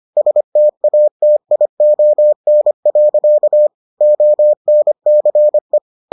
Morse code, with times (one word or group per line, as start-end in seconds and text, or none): 0.2-3.7 STATION.
4.0-5.8 ONCE